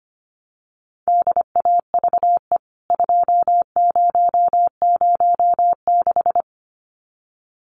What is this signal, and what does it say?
Morse code, heard as a keyed tone, about 25 wpm, 700 Hz. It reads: DAVE 2006
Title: CWT Exchange 25wpm